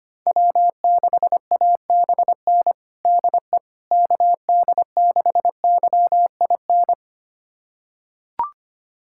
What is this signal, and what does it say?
Morse code, heard as a keyed tone, about 25 wpm, 700 Hz. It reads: W6ABN DE KD6YIN
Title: USA Callsign DE Callsign 25wpm